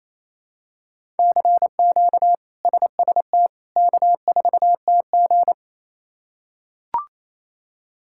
Morse code, heard as a keyed tone, about 27 wpm, 700 Hz. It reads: CQ SST K4TG
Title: K1USN SST QSO Practice - 28wpm